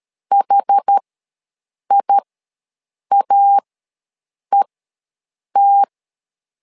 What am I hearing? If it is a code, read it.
Morse code, heard as a keyed tone, about 13 wpm, spaced as for 7 wpm, 800 Hz. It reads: HIAET
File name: LEZIONI / 2 UHDN / 2 UH ws 12wpm 1730